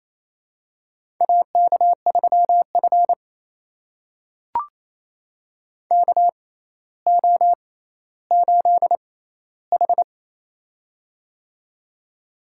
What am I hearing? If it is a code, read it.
Morse code, heard as a keyed tone, about 28 wpm, spaced as for 14 wpm, 700 Hz. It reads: AK3F KO8H
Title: USA Call Signs - 6x Character Spacing - 28wpm